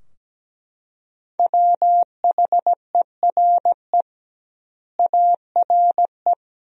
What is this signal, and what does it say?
Morse code, heard as a keyed tone, about 17 wpm, 700 Hz. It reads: WHERE ARE
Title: 2x Word-Spacing - Sentences from Top 200 Words - 17wpm